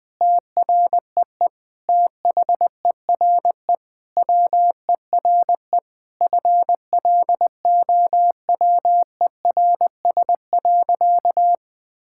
Morse code, heard as a keyed tone, about 20 wpm, 700 Hz. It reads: TREE THERE WERE FLOWERS.
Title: Sentences with 9-Word Phrases - 20wpm